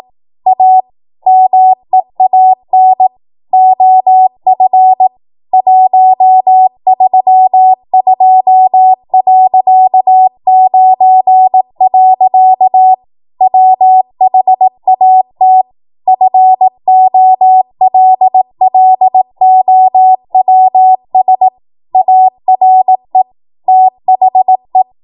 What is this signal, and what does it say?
Morse code, heard as a keyed tone, about 18 wpm, 750 Hz. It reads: A MEAN OF 132.9. WHAT FOLLOWS ARE THE
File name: Morse / 18WPM